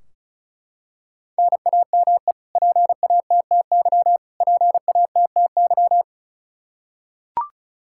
Morse code, heard as a keyed tone, about 36 wpm, 700 Hz. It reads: NAME PATTY PATTY
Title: Practice QSO - Exchange 1 - Beginning - 35wpm